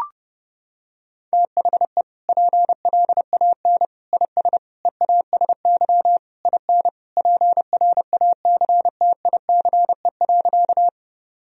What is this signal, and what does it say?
Morse code, heard as a keyed tone, about 30 wpm, 700 Hz. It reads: THE PLAN IS EASY IN PRACTICE.
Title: Sentences from Top 1000 Words - Review Part 4 - 30wpm